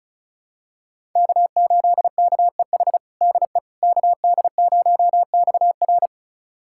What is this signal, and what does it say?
Morse code, heard as a keyed tone, about 34 wpm, 700 Hz. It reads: K8KEH DE KD0XR
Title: USA Callsign DE Callsign - 5 Characters - 35wpm